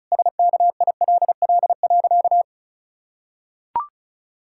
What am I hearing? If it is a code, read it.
Morse code, heard as a keyed tone, about 34 wpm, 700 Hz. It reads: SKILL.
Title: Pangram Challenge v2 - 35wpm